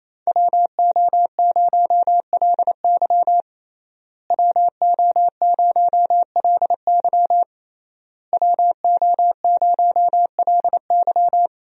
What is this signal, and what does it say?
Morse code, heard as a keyed tone, about 27 wpm, 700 Hz. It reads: WO0LY WO0LY WO0LY
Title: USA Call Signs - 5 Characters - Repeated 3x - 28wpm